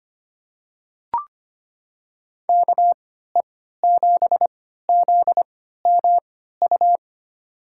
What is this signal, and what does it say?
Morse code, heard as a keyed tone, about 25 wpm, spaced as for 14 wpm, 700 Hz. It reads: KE7ZMU
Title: USA Call Signs - 3x Character Spacing - 25wpm